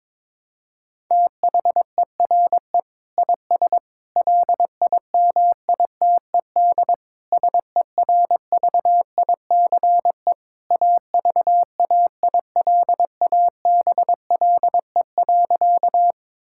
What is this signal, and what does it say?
Morse code, heard as a keyed tone, about 22 wpm, 700 Hz. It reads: THERE IS LIMITED SERVICE AVAILABLE.
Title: Sentences from Top 1000 Words - Review Part 6 - 22wpm